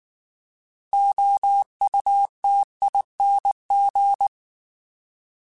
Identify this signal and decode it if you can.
Morse code, heard as a keyed tone, about 19 wpm, 800 Hz. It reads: OUTING
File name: Alphabetical / Alphabetical mp3s / Os @18